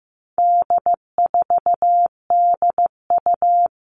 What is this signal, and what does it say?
Morse code, heard as a keyed tone, about 15 wpm, 700 Hz. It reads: D4DU
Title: All Cut Numbers - 4-Digit 15wpm